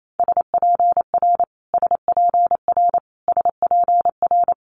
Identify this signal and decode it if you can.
Morse code, heard as a keyed tone, about 28 wpm, 700 Hz. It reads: SPR SPR SPR
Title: Multi-letter Phonemes Repeated 3x 28wpm